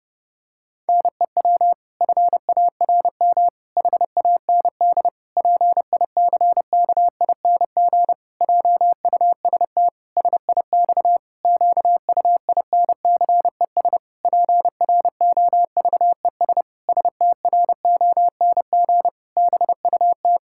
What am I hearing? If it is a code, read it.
Morse code, heard as a keyed tone, about 30 wpm, 700 Hz. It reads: NEW FARM HAND PICKING JUST SIX QUINCES PROVES STRONG BUT
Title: Pangram Challenge v2 - 30wpm